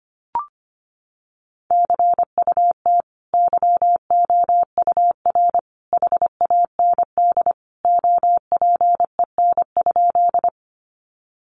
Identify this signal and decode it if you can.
Morse code, heard as a keyed tone, about 25 wpm, 700 Hz. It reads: CUT YOUR HAND OPEN?
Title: Sentences from Top 300 Words 25wpm